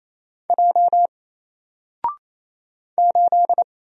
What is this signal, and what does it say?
Morse code, heard as a keyed tone, about 28 wpm, 700 Hz. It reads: J 8
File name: Single-Letter-Number-28wpm